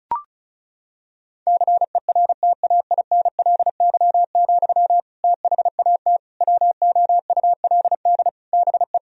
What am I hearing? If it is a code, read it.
Morse code, heard as a keyed tone, about 36 wpm, 700 Hz. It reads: CERTAINLY, THAT WOULD BE
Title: Sentences from Top 1000 Words - Review Part 2 - 35wpm